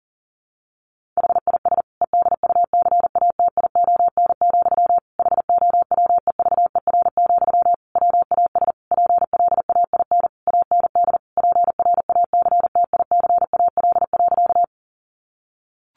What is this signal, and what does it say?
Morse code, heard as a keyed tone, about 40 wpm, 700 Hz. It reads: HIS EDUCATION, HOWEVER, WAS PLAIN AND PRACTICAL.
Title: Sentences from Top 1400 Words 40wpm